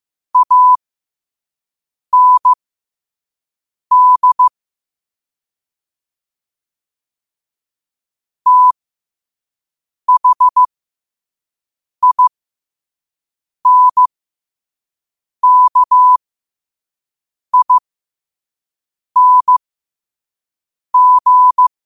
Morse code, heard as a keyed tone, about 15 wpm, spaced as for 5 wpm, 1000 Hz. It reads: AND THINKING
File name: 5wpm